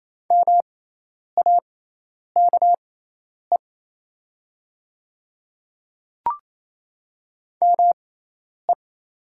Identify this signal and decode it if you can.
Morse code, heard as a keyed tone, about 28 wpm, spaced as for 10 wpm, 700 Hz. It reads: MAKE ME
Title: 6x Character Spacing - Top 500 Words - 28wpm